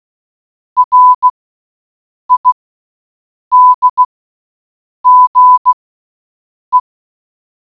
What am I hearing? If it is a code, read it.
Morse code, heard as a keyed tone, about 16 wpm, spaced as for 7 wpm, 1000 Hz. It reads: RIDGE